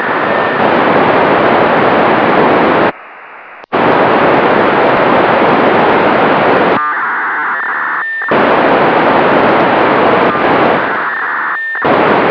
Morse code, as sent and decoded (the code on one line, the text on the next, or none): none